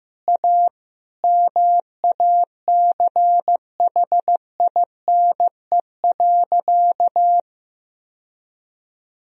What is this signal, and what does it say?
Morse code, heard as a keyed tone, about 16 wpm, 700 Hz. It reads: A MACHINE.
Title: Sentences from Top 1000 Words - Review Part 3 - 15wpm